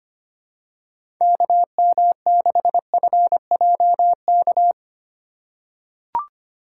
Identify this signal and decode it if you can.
Morse code, heard as a keyed tone, about 25 wpm, 700 Hz. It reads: KM6FJK
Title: USA Call Signs - 6 Characters - 2x Character Spacing - 25wpm